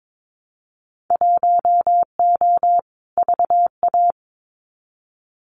Morse code, heard as a keyed tone, about 22 wpm, 700 Hz. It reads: 1O VA
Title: Winter Field Day Exchanges 22wpm